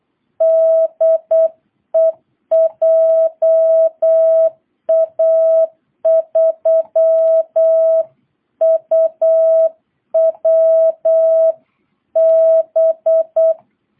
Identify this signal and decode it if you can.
Morse code, heard as a keyed tone, about 8 wpm, 660 Hz. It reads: DE JA3UWB